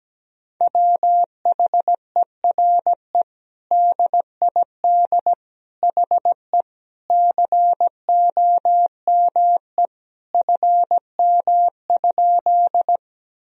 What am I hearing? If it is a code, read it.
Morse code, heard as a keyed tone, about 17 wpm, 700 Hz. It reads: WHERE DID HE COME FM?
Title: Sentences from Top 200 Words and Abbreviations 17wpm